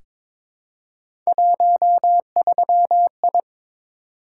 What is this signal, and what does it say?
Morse code, heard as a keyed tone, about 22 wpm, 700 Hz. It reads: 13I
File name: practice / 22wpm / Three Letter-Number Combinations - 22wpm